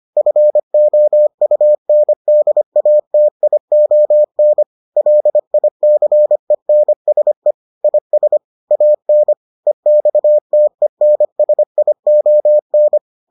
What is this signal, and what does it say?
Morse code, heard as a keyed tone, about 25 wpm, 600 Hz. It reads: FOUNDATION LICENSE IS AN EXTENSION